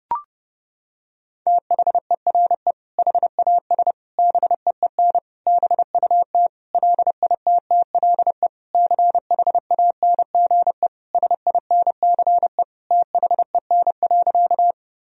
Morse code, heard as a keyed tone, about 30 wpm, 700 Hz. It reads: THERE HAS BEEN BUT LITTLE CHANGE SINCE THEN.
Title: Sentences from Top 1000 Words - Review Part 1 - 30wpm